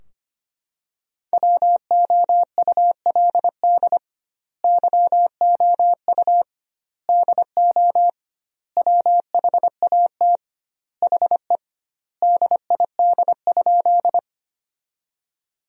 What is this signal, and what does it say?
Morse code, heard as a keyed tone, about 25 wpm, 700 Hz. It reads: WOULD YOU DO WHAT HE DID?
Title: 2x Word-Spacing - Sentences from Top 100 Words - 25wpm